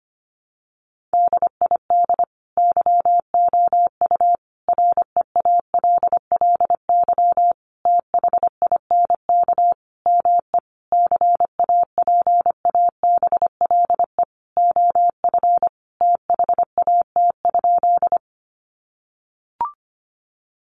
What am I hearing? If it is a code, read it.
Morse code, heard as a keyed tone, about 25 wpm, 700 Hz. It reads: DID YOU REALLY THINK ME CAPABLE OF THAT?
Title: Sentences from Top 1200 Words 25wpm